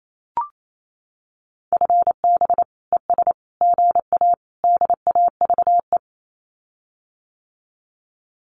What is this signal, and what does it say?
Morse code, heard as a keyed tone, about 28 wpm, 700 Hz. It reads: FB ES GA DAVE
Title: Common QSO Phrases Encore 28wpm